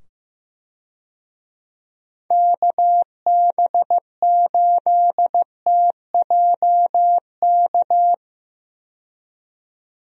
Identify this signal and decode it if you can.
Morse code, heard as a keyed tone, about 15 wpm, 700 Hz. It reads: KB8TJK
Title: USA Call Signs - 6 Characters - 15wpm